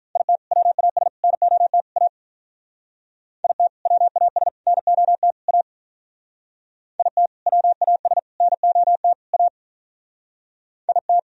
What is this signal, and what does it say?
Morse code, heard as a keyed tone, about 44 wpm, 700 Hz. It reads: IT WAS NOT A IT WAS NOT A IT WAS NOT A IT